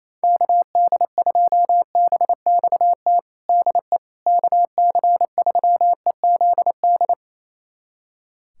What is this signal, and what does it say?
Morse code, heard as a keyed tone, about 28 wpm, 700 Hz. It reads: KD2BXT DE KC3EZD
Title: USA Callsign DE Callsign 28wpm